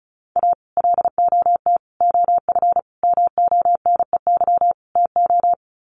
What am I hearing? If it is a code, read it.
Morse code, heard as a keyed tone, about 34 wpm, 700 Hz. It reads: A LOT OF MONEY TO
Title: Sets of 5 Words Encore2 35wpm